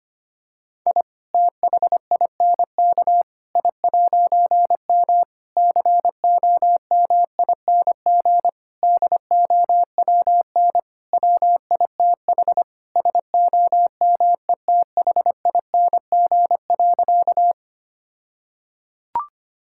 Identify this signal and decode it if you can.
Morse code, heard as a keyed tone, about 25 wpm, 700 Hz. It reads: I THINK I'M COMING DOWN WITH SOMETHING.
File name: practice / 25wpm / All-Age Appropriate Jokes - 25wpm